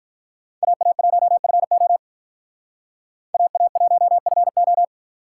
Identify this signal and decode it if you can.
Morse code, heard as a keyed tone, about 46 wpm, 700 Hz. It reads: AA1FO AA1FO